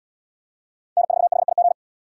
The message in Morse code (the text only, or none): N4HIG